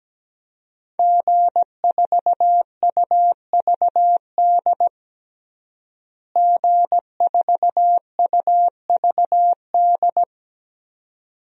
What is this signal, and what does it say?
Morse code, heard as a keyed tone, about 17 wpm, 700 Hz. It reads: G4UVD G4UVD
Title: DXCC-100 Call Signs - 5 Characters - Repeated 2x - 17wpm